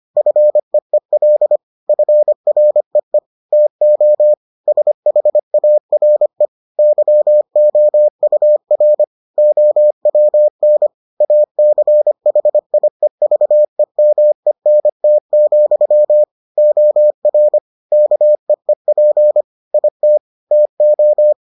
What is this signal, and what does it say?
Morse code, heard as a keyed tone, about 25 wpm, 600 Hz. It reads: FEEL FREE TO SHARE YOUR OWN ACHIEVEMENT, OR KEEP IT TO